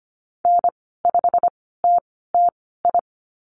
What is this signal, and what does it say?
Morse code, heard as a keyed tone, about 25 wpm, spaced as for 16 wpm, 700 Hz. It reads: N5TTI
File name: Lesson 15g - Number 5 - Call Signs 25wpm